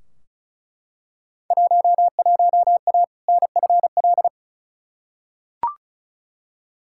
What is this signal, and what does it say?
Morse code, heard as a keyed tone, about 36 wpm, 700 Hz. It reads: 11A NFL
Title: Field Day Exchanges Encore - 35wpm